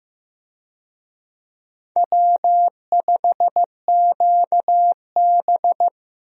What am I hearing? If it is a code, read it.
Morse code, heard as a keyed tone, about 16 wpm, 700 Hz. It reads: W5QB
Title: USA Call Signs - 4 Characters - Repeated 3x - 15wpm